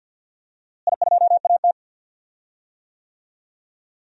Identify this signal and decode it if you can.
Morse code, heard as a keyed tone, about 49 wpm, 700 Hz. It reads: I2AT
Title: DXCC-100 Call Signs - 4 Characters - Repeated 2x - 50wpm